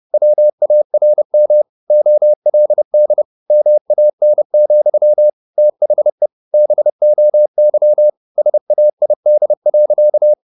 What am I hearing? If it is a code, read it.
Morse code, heard as a keyed tone, about 30 wpm, 600 Hz. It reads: WARM OLD MAN, THE BOY SAID.
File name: the old man and the sea / the old man and the sea 1 30wpm